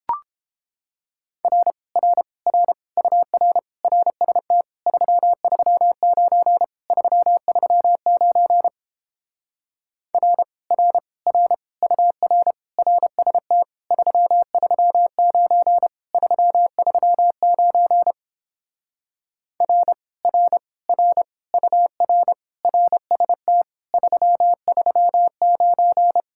Morse code, heard as a keyed tone, about 29 wpm, 700 Hz. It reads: R R R UR RST 339 339 R R R UR RST 339 339 R R R UR RST 339